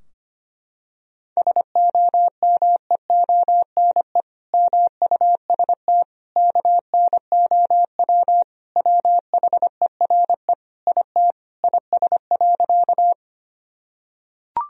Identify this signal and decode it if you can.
Morse code, heard as a keyed tone, about 25 wpm, 700 Hz. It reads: SOMEONE MUST KNOW WHERE IT IS.
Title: Sentences from Top 1000 Words - Review Part 4 - 25wpm